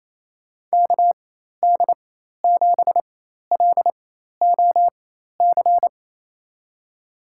Morse code, heard as a keyed tone, about 27 wpm, spaced as for 13 wpm, 700 Hz. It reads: KD7LOC
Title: USA Call Signs - 6 Characters - 4x Character Spacing - 28wpm